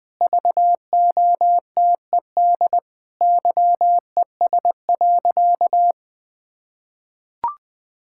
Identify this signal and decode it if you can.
Morse code, heard as a keyed tone, about 20 wpm, 700 Hz. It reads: VOTED YES.
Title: Sentences with 6-Word Phrases - 20wpm